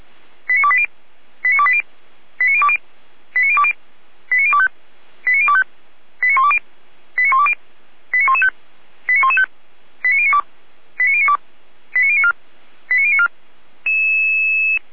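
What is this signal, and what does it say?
Five-tone selective call (ZVEI), heard as quick runs of five tones, tones of about 70 ms. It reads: 89280 89280 89020 89020 89025 89025 89120 89120 89115 89115 89092 89092 89094 89094
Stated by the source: ZVEI five-tone selective call 89262 received by SDR and logged by Feuerwehr Orlamünde, a fire brigade.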